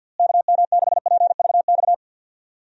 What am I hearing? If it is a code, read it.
Morse code, heard as a keyed tone, about 50 wpm, 700 Hz. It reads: KM6PVX
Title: USA Call Signs - 6 Characters - Repeated 3x - 50wpm